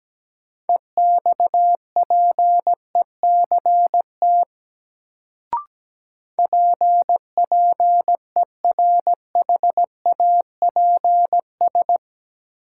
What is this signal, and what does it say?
Morse code, heard as a keyed tone, about 17 wpm, 700 Hz. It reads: EXPECT PPERHAPS
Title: Sentences from Top 1100 Words - New Word Drill - 17wpm